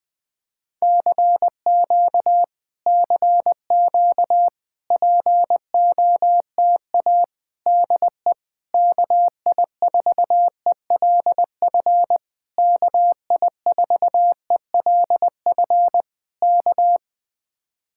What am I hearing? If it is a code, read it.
Morse code, heard as a keyed tone, about 20 wpm, 700 Hz. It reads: CQ CQ POTA DE KI4ELF KI4ELF K
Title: Speed-Racing Common QSO Phrases Encore - 20wpm